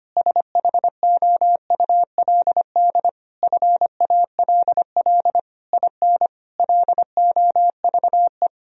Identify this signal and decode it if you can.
Morse code, heard as a keyed tone, about 25 wpm, 700 Hz. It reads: SHOULD FALL IN LOVE
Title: Sentences from Top 1000 Words - Review Part 4 - 25wpm